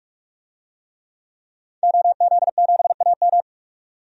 Morse code, heard as a keyed tone, about 46 wpm, 700 Hz. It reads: OZ7AM
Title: DXCC-100 Call Signs - 5 Characters - 45wpm